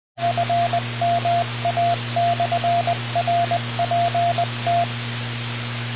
Morse code, heard as a keyed tone, about 20 wpm, 690 Hz, with radio noise. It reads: FMA/RPT
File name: train net / rainN-002